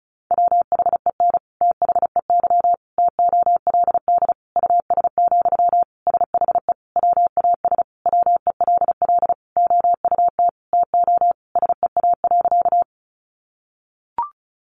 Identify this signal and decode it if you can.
Morse code, heard as a keyed tone, about 34 wpm, 700 Hz. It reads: WHEN THEY TOLD US, SHE WAS WELL OUT TO SEA.
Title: Sentences from Top 400 Words 35wpm